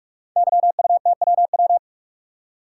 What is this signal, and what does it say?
Morse code, heard as a keyed tone, about 44 wpm, 700 Hz. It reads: YUTWW
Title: Code Groups - 5 Letters - 45wpm